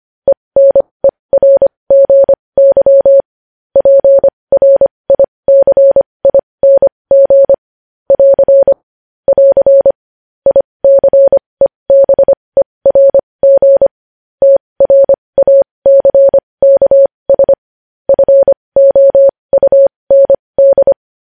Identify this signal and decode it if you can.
Morse code, heard as a keyed tone, about 25 wpm, spaced as for 20 wpm, 550 Hz. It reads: ENERGY PRICING <AR> <AR> ICEBERG TRACKS FOUND
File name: podcast / News headlines-1x-25x20